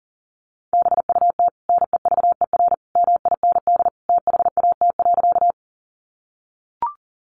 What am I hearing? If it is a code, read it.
Morse code, heard as a keyed tone, about 40 wpm, 700 Hz. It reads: BUT NEVER MIND THAT.
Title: Sentences from Top 500 Words 40wpm